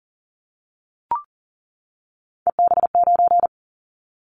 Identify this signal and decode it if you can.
Morse code, heard as a keyed tone, about 38 wpm, 690 Hz. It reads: EB9
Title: Callsign Prefixes for Top 100 DXCC - 40wpm